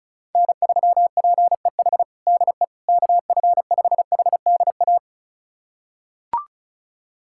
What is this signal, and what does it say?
Morse code, heard as a keyed tone, about 34 wpm, 700 Hz. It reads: N3PEH DE KF5HDA
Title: USA Callsign DE Callsign 35wpm